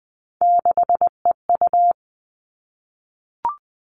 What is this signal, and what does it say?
Morse code, heard as a keyed tone, about 20 wpm, 700 Hz. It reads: 6EU
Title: All Cut Numbers - 3-Digit 20wpm